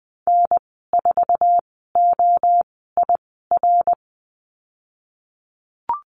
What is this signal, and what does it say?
Morse code, heard as a keyed tone, about 20 wpm, spaced as for 15 wpm, 700 Hz. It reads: N4OIR
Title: 055 Lesson 10g - Letter r - Call Signs 20wpm